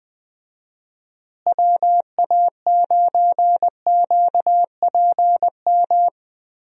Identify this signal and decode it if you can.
Morse code, heard as a keyed tone, about 19 wpm, 700 Hz. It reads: WA9QPM
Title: USA Call Signs - 6 Characters - Repeated 3x - 20wpm